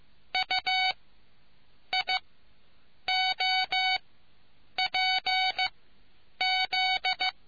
Morse code, heard as a keyed tone, about 15 wpm, spaced as for 7 wpm, 2250 Hz. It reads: UIOPZ